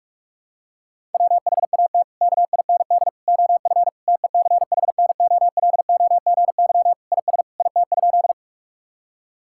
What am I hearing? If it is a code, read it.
Morse code, heard as a keyed tone, about 41 wpm, 700 Hz. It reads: WHAT KIND OF TECHNOLOGY IS IT?
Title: Sentences from Top 1000 Words - Review Part 5 - 45wpm